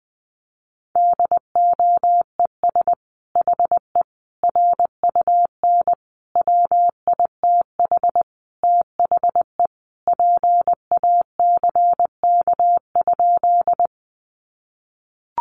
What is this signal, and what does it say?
Morse code, heard as a keyed tone, about 20 wpm, 700 Hz. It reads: DOES HE RUN WITH THE PACK?
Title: Sentences from Top 1300 Words 20wpm